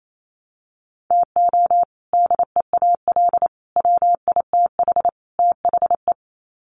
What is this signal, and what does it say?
Morse code, heard as a keyed tone, about 28 wpm, 700 Hz. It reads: TO DEAL WITH THE
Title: Sets of 4 Words 28wpm